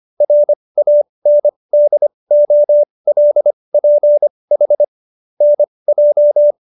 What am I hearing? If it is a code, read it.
Morse code, heard as a keyed tone, about 25 wpm, spaced as for 20 wpm, 590 Hz. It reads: RANDOLPH NJ